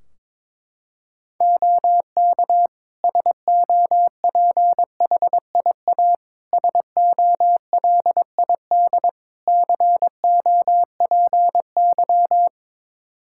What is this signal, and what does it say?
Morse code, heard as a keyed tone, about 22 wpm, 700 Hz. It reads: OK SOPHIA SOLID COPY
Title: Practice QSO - Exchange 3 - More Info - 22wpm